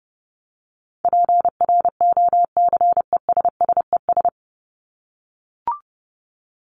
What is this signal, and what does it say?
Morse code, heard as a keyed tone, about 30 wpm, 700 Hz. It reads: PROCESSES
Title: ES:S Suffix - Top 100 Words with Base Words 30wpm